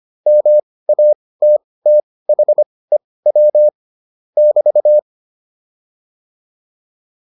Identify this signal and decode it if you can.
Morse code, heard as a keyed tone, about 25 wpm, spaced as for 18 wpm, 600 Hz. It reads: MATTHEW <BT>
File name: QSO308 18